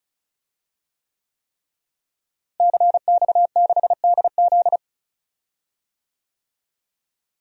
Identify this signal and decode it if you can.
Morse code, heard as a keyed tone, about 34 wpm, 700 Hz. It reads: CX6DZ
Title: DXCC-100 Call Signs - 5 Characters - Repeated 3x - 35wpm